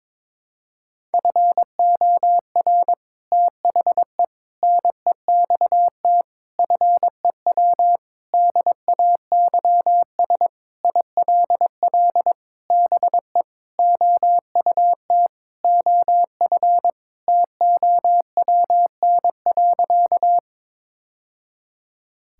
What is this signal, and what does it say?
Morse code, heard as a keyed tone, about 22 wpm, 700 Hz. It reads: FOR THE NEXT FEW DAYS ILL BE OUT OF TOWN.
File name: practice / 22wpm / Sentences with 3-Word Phrases - 22wpm